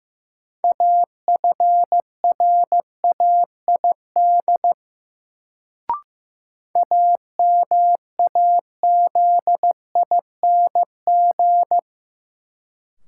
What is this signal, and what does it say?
Morse code, heard as a keyed tone, about 15 wpm, 700 Hz. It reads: AFRAID AMAZING
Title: Sentences from Top 1500 Words - New Word Drill Alt - 15wpm